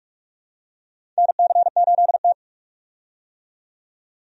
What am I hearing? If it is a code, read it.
Morse code, heard as a keyed tone, about 45 wpm, 700 Hz. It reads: NC8T
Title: USA Call Signs - 4 Characters - 45wpm